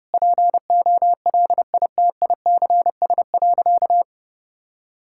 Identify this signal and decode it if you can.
Morse code, heard as a keyed tone, about 28 wpm, 700 Hz. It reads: POLITICS.
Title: Sentences from Top 1000 Words - Review Part 6 - 30wpm